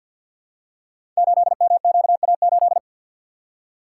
Morse code, heard as a keyed tone, about 53 wpm, 700 Hz. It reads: 8MQA8
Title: Code Groups - 5 Letter:Numbers - 50wpm